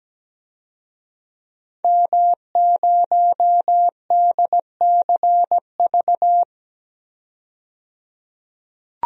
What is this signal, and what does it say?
Morse code, heard as a keyed tone, about 17 wpm, 700 Hz. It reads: M0DCV
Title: DXCC-100 Call Signs - 5 Characters - 17wpm